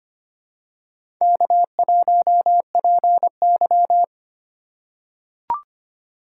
Morse code, heard as a keyed tone, about 25 wpm, 700 Hz. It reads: K1PY
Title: USA Call Signs - 4 Characters - 3x Character Spacing - 25wpm